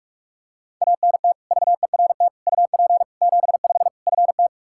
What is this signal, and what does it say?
Morse code, heard as a keyed tone, about 47 wpm, 700 Hz. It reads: ANT VERT UP 75 FT
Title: K1USN SST QSO Practice 45wpm